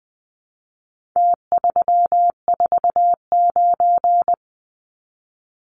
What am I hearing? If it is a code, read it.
Morse code, heard as a keyed tone, about 20 wpm, 700 Hz. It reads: T349
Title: Numbers - Contest Style - 4-Digit 20wpm